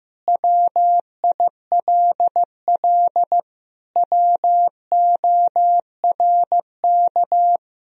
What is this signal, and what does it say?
Morse code, heard as a keyed tone, about 15 wpm, 700 Hz. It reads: WILL WORK
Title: Sentences from Top 1000 Words - Review Part 2 - 15wpm